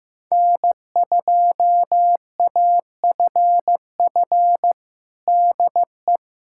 Morse code, heard as a keyed tone, about 15 wpm, 700 Hz. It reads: N2AFF DE
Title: USA Callsign DE Callsign - 5 Characters - 15wpm